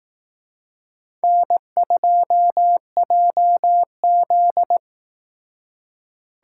Morse code, heard as a keyed tone, about 18 wpm, 700 Hz. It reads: N2JZ
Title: Speed Racing - USA 1x2 Call Signs 15wpm